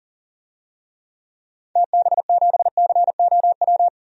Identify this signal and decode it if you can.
Morse code, heard as a keyed tone, about 40 wpm, 700 Hz. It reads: TB7COW
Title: DXCC-100 Call Signs - 6 Characters - Repeated 2x - 40wpm